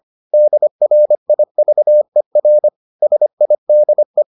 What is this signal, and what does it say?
Morse code, heard as a keyed tone, about 25 wpm, 600 Hz. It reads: DRIVER SIDE